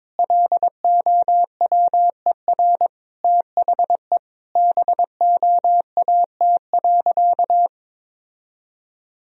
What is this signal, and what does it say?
Morse code, heard as a keyed tone, about 22 wpm, 700 Hz. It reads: LOWER THE BOAT.
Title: Sentences from Top 1000 Words - Review Part 4 - 22wpm